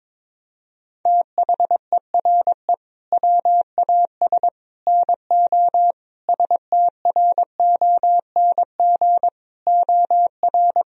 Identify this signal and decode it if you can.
Morse code, heard as a keyed tone, about 22 wpm, 700 Hz. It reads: THERE WAS NO STRONG OR
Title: Sentences from Top 1000 Words - Review Part 4 - 22wpm